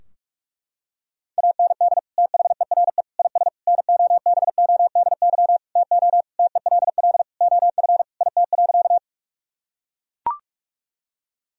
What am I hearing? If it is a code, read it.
Morse code, heard as a keyed tone, about 43 wpm, 700 Hz. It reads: AND THERE IS NOBODY TO TELL OF IT.
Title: Sentences from Top 1600 Words 45wpm